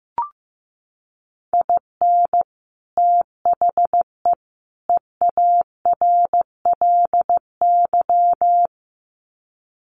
Morse code, heard as a keyed tone, about 15 wpm, 700 Hz. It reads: IN THE EARLY
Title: Sets of 3 Words 15wpm